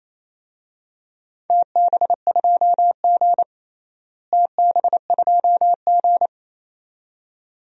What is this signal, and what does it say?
Morse code, heard as a keyed tone, about 28 wpm, 700 Hz. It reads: TB2G TB2G
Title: DXCC-100 Call Signs - 4 Characters - Repeated 2x - 28wpm